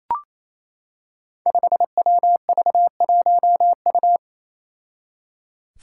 Morse code, heard as a keyed tone, about 29 wpm, 700 Hz. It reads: 5WV1U
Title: Five Challenging Letter-Number Combinations 28wpm